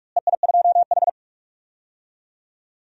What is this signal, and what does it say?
Morse code, heard as a keyed tone, about 46 wpm, 700 Hz. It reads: EI2H
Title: DXCC-100 Call Signs - 4 Characters - Repeated 3x - 45wpm